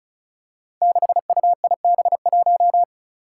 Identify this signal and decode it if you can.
Morse code, heard as a keyed tone, about 36 wpm, 700 Hz. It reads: 6UIB1
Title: Five Challenging Letter-Number Combinations 35wpm